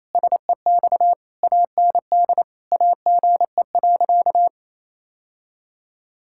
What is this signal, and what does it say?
Morse code, heard as a keyed tone, about 28 wpm, 700 Hz. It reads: SEX AND AGE.
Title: Sentences from Top 1600 Words 28wpm